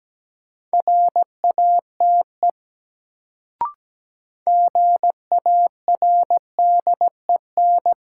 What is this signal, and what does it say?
Morse code, heard as a keyed tone, about 17 wpm, 700 Hz. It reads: RATE GARDEN
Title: Sentences from Top 700 Words - New Word Drill - 17wpm